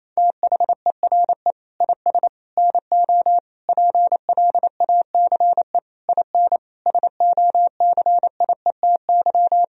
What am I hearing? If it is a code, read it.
Morse code, heard as a keyed tone, about 28 wpm, 700 Hz. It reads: THERE IS NO PLACE IN SOCIETY
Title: Sentences from Top 1000 Words - Review Part 4 - 28wpm